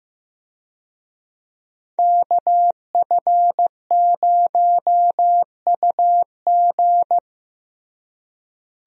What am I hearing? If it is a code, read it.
Morse code, heard as a keyed tone, about 16 wpm, 700 Hz. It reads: KF0UG
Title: USA Call Signs - 5 Characters - Repeated 3x - 15wpm